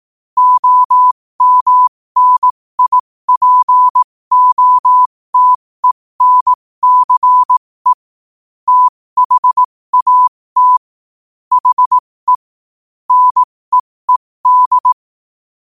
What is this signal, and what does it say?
Morse code, heard as a keyed tone, about 18 wpm, spaced as for 15 wpm, 1000 Hz. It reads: OMNIPOTENCE THAT HE NEED
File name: qotdfiles / 15wpm